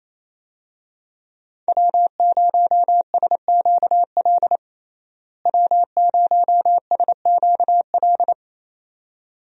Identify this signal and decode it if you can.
Morse code, heard as a keyed tone, about 27 wpm, 700 Hz. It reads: W0SQL W0SQL
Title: USA Call Signs - 5 Characters - Repeated 2x - 28wpm